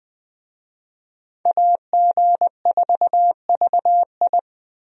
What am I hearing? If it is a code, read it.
Morse code, heard as a keyed tone, about 21 wpm, 700 Hz. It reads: AG4VI
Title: USA Call Signs - 5 Characters - 20wpm